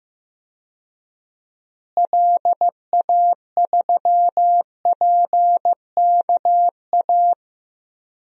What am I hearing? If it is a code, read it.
Morse code, heard as a keyed tone, about 15 wpm, 700 Hz. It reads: LA3PKA
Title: DXCC-100 Call Signs - 6 Characters - Repeated 2x - 15wpm